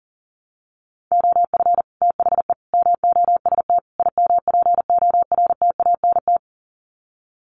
Morse code, heard as a keyed tone, about 40 wpm, 700 Hz. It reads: OF THE MOST IMPORTANT
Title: Sets of 4 Words 40wpm